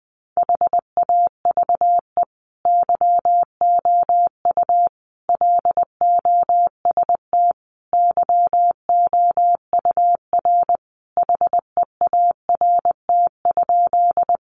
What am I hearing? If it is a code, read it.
Morse code, heard as a keyed tone, about 20 wpm, 700 Hz. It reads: HAVE YOU LOST YOUR HEART?
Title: Sentences from Top 1400 Words 20wpm